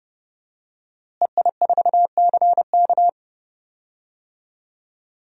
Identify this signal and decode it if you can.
Morse code, heard as a keyed tone, about 30 wpm, 700 Hz. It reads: EI4CK
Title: DXCC-100 Call Signs - 5 Characters - Repeated 3x - 30wpm